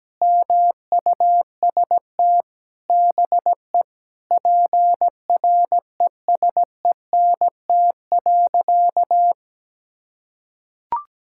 Morse code, heard as a keyed tone, about 17 wpm, 700 Hz. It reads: MUST BE PRESENT.
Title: Sentences from Top 1600 Words 17wpm